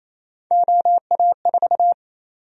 Morse code, heard as a keyed tone, about 27 wpm, 700 Hz. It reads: OA4
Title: USA Call Sign Prefixes - 28wpm